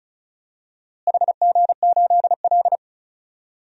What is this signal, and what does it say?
Morse code, heard as a keyed tone, about 36 wpm, 700 Hz. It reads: HG8L
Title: DXCC-100 Call Signs - 4 Characters - Repeated 3x - 35wpm